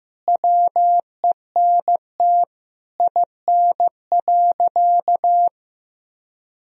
Morse code, heard as a keyed tone, about 15 wpm, 700 Hz. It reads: WENT IN.
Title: Sentences from Top 300 Words and Abbreviations 15wpm